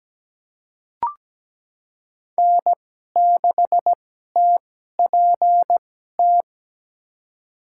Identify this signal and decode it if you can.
Morse code, heard as a keyed tone, about 17 wpm, spaced as for 12 wpm, 700 Hz. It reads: N6TPT
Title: USA Call Signs - 5 Characters - 2x Character Spacing - 17wpm